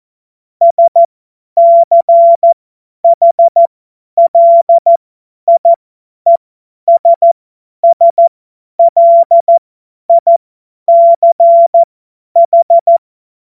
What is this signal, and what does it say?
Morse code, heard as a keyed tone, about 14 wpm, spaced as for 10 wpm, 680 Hz. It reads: SCHLIESSLICH
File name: Morsekurs / Witze / Witz 01. August 2023 50 F70